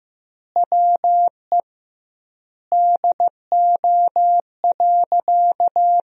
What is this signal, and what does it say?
Morse code, heard as a keyed tone, about 15 wpm, 700 Hz. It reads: WE DO.
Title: 2x Word-Spacing - Sentences from Top 100 Words - 15wpm